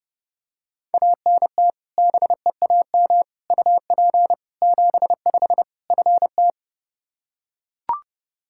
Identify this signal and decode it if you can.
Morse code, heard as a keyed tone, about 30 wpm, 700 Hz. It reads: ANT BEAM UP 75 FT
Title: Practice QSO - Exchange 2 - Info - 30wpm